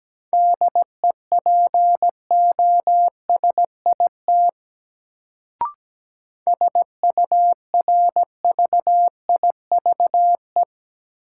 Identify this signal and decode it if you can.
Morse code, heard as a keyed tone, about 17 wpm, 700 Hz. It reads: DEPOSIT SURVIVE
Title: Sentences from Top 1300 Words - New Word Drill Alt - 17wpm